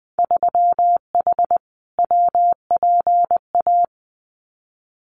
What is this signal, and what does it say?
Morse code, heard as a keyed tone, about 20 wpm, 700 Hz. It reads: 3H WPA
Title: Winter Field Day Exchanges 20wpm